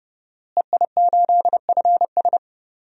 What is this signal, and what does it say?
Morse code, heard as a keyed tone, about 32 wpm, 700 Hz. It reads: EI8FS
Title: DXCC-100 Call Signs - 5 Characters - Repeated 3x - 30wpm